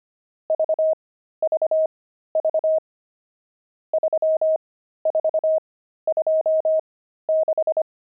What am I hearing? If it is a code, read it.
Morse code, heard as a keyed tone, about 25 wpm, spaced as for 13 wpm, 630 Hz. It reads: VVV 3426
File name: pangram-fs-13-cs-25-nums9